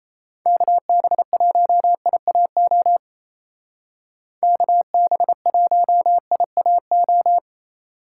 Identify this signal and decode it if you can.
Morse code, heard as a keyed tone, about 35 wpm, 700 Hz. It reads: KB1IAO KB1IAO
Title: Speed Racing - USA 2x3 Call Signs 28wpm